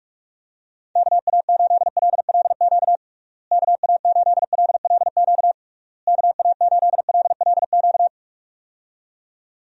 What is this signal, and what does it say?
Morse code, heard as a keyed tone, about 46 wpm, 700 Hz. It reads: KA8LLQ KA8LLQ KA8LLQ
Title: USA Call Signs - 6 Characters - Repeated 3x - 45wpm